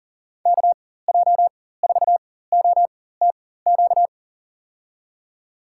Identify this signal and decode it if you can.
Morse code, heard as a keyed tone, about 40 wpm, spaced as for 19 wpm, 700 Hz. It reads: KJ4OTQ
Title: USA Call Signs - 6 Characters - 4x Character Spacing - 40wpm